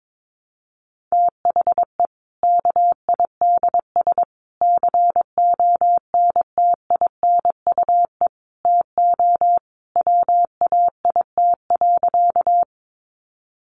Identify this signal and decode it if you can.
Morse code, heard as a keyed tone, about 22 wpm, 700 Hz. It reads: THE KIDS CONTINUE TO WAIT.
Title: Easy Phrases 22wpm